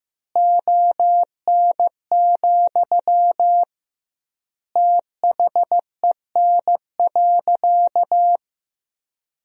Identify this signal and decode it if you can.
Morse code, heard as a keyed tone, about 15 wpm, 700 Hz. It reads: ON, THEN.
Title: 2x Word-Spacing - Sentences from Top 200 Words - 15wpm